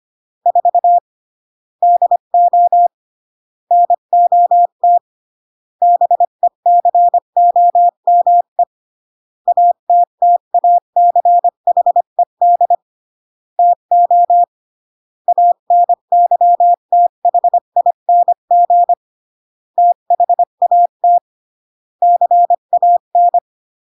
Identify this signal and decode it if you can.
Morse code, heard as a keyed tone, about 25 wpm, 700 Hz. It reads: 4 DO NOT BECOME ATTACHED TO ANYTHING THAT CAN